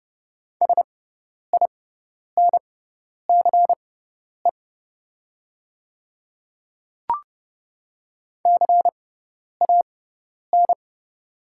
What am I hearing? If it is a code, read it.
Morse code, heard as a keyed tone, about 30 wpm, spaced as for 10 wpm, 700 Hz. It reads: SINCE CAN
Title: 6x Character Spacing - Top 300 Words - 30wpm